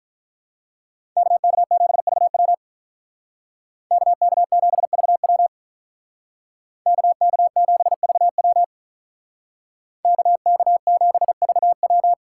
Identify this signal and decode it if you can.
Morse code, heard as a keyed tone, about 43 wpm, 700 Hz. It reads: KK7VW KK7VW KK7VW KK7VW
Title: Speed Racing - USA 2x2 Call Signs 35wpm